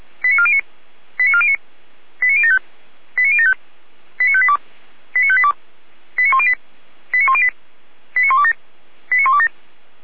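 Five-tone selective call (ZVEI), heard as quick runs of five tones, tones of about 70 ms. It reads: 89409 89409 89075 89075 89582 89582 89108 89108 89127 89127
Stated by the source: ZVEI five-tone selective call received on SDR.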